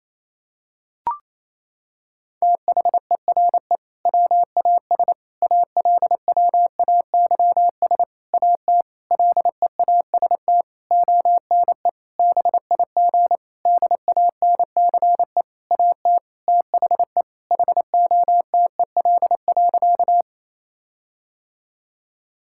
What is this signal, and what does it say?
Morse code, heard as a keyed tone, about 28 wpm, 700 Hz. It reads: THERE WAS ALWAYS AT LEAST ONE BIG DANCE AT THE HOTEL.
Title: Sentences from Top 1600 Words 28wpm